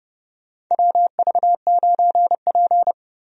Morse code, heard as a keyed tone, about 32 wpm, 700 Hz. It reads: WV9P
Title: USA Call Signs - 4 Characters - 30wpm